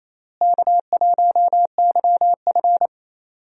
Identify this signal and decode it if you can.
Morse code, heard as a keyed tone, about 28 wpm, 700 Hz. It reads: K1YF
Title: USA Call Signs - 4 Characters - Repeated 3x - 28wpm